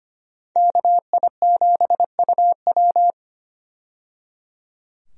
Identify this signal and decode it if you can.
Morse code, heard as a keyed tone, about 25 wpm, 700 Hz. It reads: KI7UW
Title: USA Call Signs - 5 Characters - 25wpm